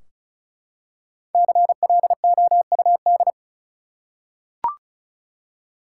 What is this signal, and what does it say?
Morse code, heard as a keyed tone, about 34 wpm, 700 Hz. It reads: CLOUD
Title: Five Letter Words - 35wpm